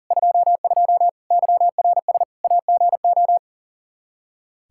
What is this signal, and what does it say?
Morse code, heard as a keyed tone, about 40 wpm, 700 Hz. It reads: 22 YRS AGO
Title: Practice QSO - Exchange 3 - More Info - 40wpm